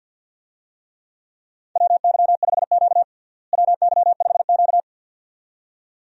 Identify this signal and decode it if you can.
Morse code, heard as a keyed tone, about 47 wpm, 700 Hz. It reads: WY5Q WY5Q
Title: USA Call Signs - 4 Characters - Repeated 2x - 50wpm